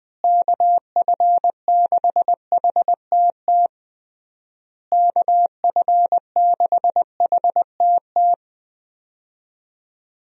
Morse code, heard as a keyed tone, about 21 wpm, 700 Hz. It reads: KF6HTT KF6HTT
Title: USA Call Signs - 6 Characters - Repeated 3x - 20wpm